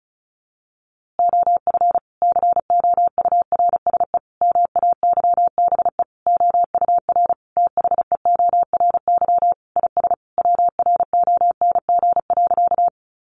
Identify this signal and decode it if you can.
Morse code, heard as a keyed tone, about 34 wpm, 700 Hz. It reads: OF COURSE MAYBE OUR THEORY IS WRONG.
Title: A Sound Of Thunder 35wpm